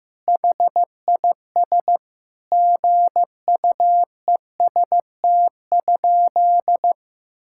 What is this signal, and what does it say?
Morse code, heard as a keyed tone, about 15 wpm, 700 Hz. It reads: HIS GUEST?
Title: Sentences from Top 1600 Words 15wpm